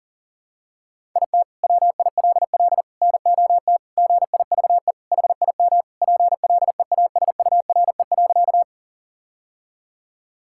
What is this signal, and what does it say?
Morse code, heard as a keyed tone, about 40 wpm, 700 Hz. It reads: IT WILL NOT GIVE HIM PLEASURE.
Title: Sentences from Top 1600 Words 40wpm